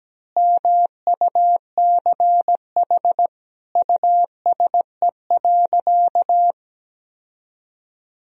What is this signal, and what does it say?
Morse code, heard as a keyed tone, about 17 wpm, 700 Hz. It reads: MUCH USE.
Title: Sentences from Top 1000 Words - Review Part 3 - 17wpm